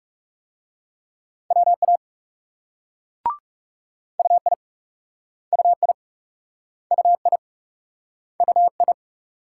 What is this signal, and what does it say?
Morse code, heard as a keyed tone, about 41 wpm, 700 Hz. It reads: WA UI UI UI UI